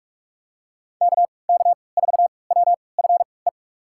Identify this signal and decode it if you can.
Morse code, heard as a keyed tone, about 46 wpm, spaced as for 25 wpm, 700 Hz. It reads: KK4WFE
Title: USA Call Signs - 6 Characters - 3x Character Spacing - 45wpm